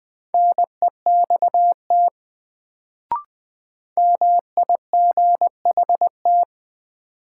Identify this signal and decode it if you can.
Morse code, heard as a keyed tone, about 20 wpm, 700 Hz. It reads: NEXT MIGHT
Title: Sentences from Top 300 Words - New Word Drill - 20wpm